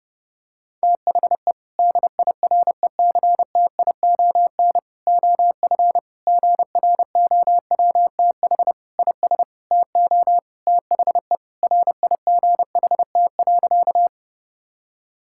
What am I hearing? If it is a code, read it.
Morse code, heard as a keyed tone, about 32 wpm, 700 Hz. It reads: THE DIRECTION OF GROWTH IS TO THE RIGHT.
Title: Sentences from Top 1000 Words - Review Part 5 - 30wpm